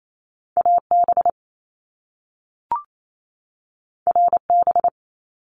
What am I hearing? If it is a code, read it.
Morse code, heard as a keyed tone, about 28 wpm, 700 Hz. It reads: AB RB
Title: Top Two Letter Combinations in Words 28wpm